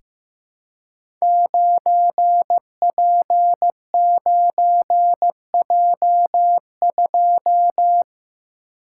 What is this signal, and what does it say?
Morse code, heard as a keyed tone, about 15 wpm, 700 Hz. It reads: 9P9J2
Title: Code Groups - 5 Letter:Numbers - 15wpm